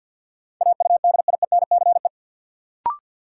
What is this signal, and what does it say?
Morse code, heard as a keyed tone, about 47 wpm, 700 Hz. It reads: AUDIENCE
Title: Sentences from Top 800 Words - New Word Drill - 50wpm